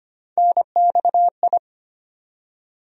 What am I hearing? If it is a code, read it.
Morse code, heard as a keyed tone, about 25 wpm, 700 Hz. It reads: NXI
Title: USA Call Sign Postfixes - 3 Letters 25wpm